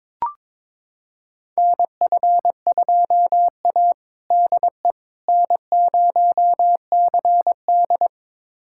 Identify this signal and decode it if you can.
Morse code, heard as a keyed tone, about 21 wpm, 700 Hz. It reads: NF2A DE N0CD
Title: USA Callsign DE Callsign - 4 Characters - 22wpm